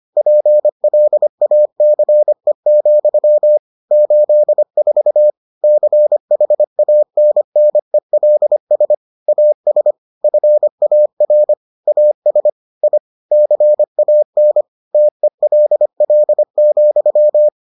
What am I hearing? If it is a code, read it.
Morse code, heard as a keyed tone, about 25 wpm, 600 Hz. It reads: PLACE, 84 CHANNELS AS FAR AS I CAN TELL,